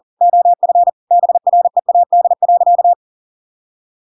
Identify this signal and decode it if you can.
Morse code, heard as a keyed tone, about 40 wpm, 700 Hz. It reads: OF BREAD.
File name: Pinocchio / Pinocchio ch5 40 40 ik1hge